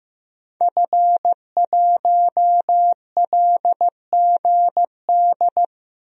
Morse code, heard as a keyed tone, about 15 wpm, 700 Hz. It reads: F1LGD
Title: DXCC-100 Call Signs - 5 Characters - Repeated 3x - 15wpm